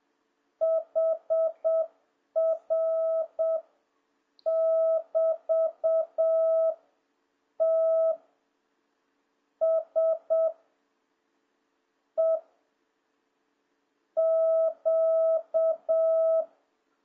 Morse code, heard as a keyed tone, about 7 wpm, 660 Hz. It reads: HR <BT> T S E Q